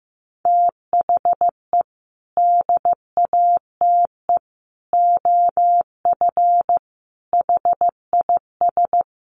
Morse code, heard as a keyed tone, about 15 wpm, 700 Hz. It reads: THE DATE OF HIS
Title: Sentences from Top 1100 Words 15wpm